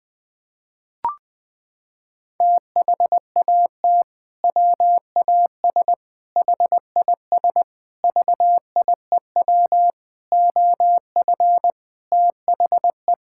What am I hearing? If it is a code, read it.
Morse code, heard as a keyed tone, about 20 wpm, 700 Hz. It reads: THAT WAS HIS VIEW OF THE
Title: Sentences from Top 1000 Words - Review Part 6 - 20wpm